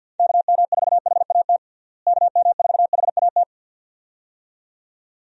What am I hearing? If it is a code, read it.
Morse code, heard as a keyed tone, about 50 wpm, 700 Hz. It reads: KM4HAT KM4HAT
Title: USA Call Signs - 6 Characters - Repeated 2x - 50wpm